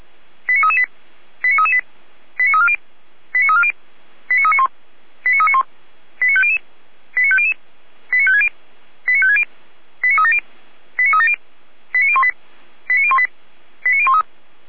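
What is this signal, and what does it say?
Five-tone selective call (ZVEI), heard as quick runs of five tones, tones of about 70 ms. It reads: 89338 89338 89340 89340 89391 89391 89500 89500 89560 89560 89370 89370 89018 89018 89013
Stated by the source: ZVEI five-tone selective call received on SDR.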